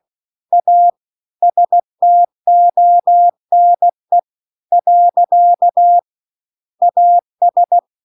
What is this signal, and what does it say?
Morse code, heard as a keyed tone, about 16 wpm, 700 Hz. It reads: A STONE . AS
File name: CW-The-Big-Twoheaded-River-2-16WPM